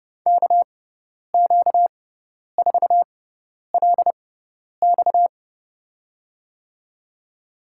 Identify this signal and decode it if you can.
Morse code, heard as a keyed tone, about 30 wpm, spaced as for 10 wpm, 700 Hz. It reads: KQ4LX
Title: USA Call Signs - 6x Character Spacing - 30wpm